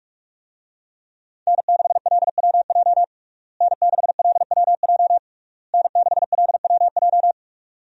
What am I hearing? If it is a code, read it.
Morse code, heard as a keyed tone, about 46 wpm, 700 Hz. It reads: N6LWJ N6LWJ N6LWJ
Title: USA Call Signs - 5 Characters - Repeated 3x - 45wpm